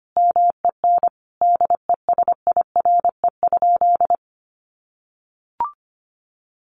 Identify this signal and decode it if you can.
Morse code, heard as a keyed tone, about 25 wpm, 700 Hz. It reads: MEN DESIRE?
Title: Sentences from Top 1100 Words 25wpm